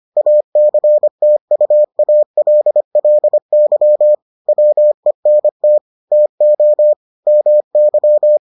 Morse code, heard as a keyed tone, about 25 wpm, 600 Hz. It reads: ACTUALLY WENT TO MY